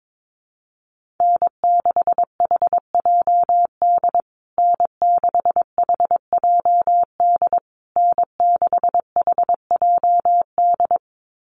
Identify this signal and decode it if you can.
Morse code, heard as a keyed tone, about 22 wpm, 700 Hz. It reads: N6HJD N6HJD N6HJD
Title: USA Call Signs Repeated 3x 22wpm